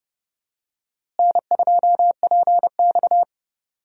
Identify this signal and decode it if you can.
Morse code, heard as a keyed tone, about 32 wpm, 700 Hz. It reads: N2PX
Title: USA Call Signs - 4 Characters - Repeated 3x - 30wpm